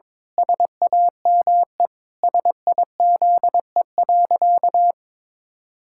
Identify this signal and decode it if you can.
Morse code, heard as a keyed tone, about 22 wpm, 700 Hz. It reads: SAME SIZE.
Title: Pangram Challenge v2 - 22wpm